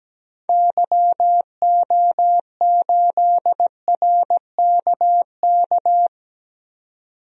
Morse code, heard as a keyed tone, about 17 wpm, 700 Hz. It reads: YO8RKK
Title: DXCC-100 Call Signs - 6 Characters - 17wpm